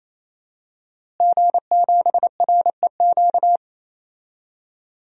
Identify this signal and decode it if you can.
Morse code, heard as a keyed tone, about 28 wpm, 700 Hz. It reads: G7REQ
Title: DXCC-100 Call Signs - 5 Characters - Repeated 3x - 28wpm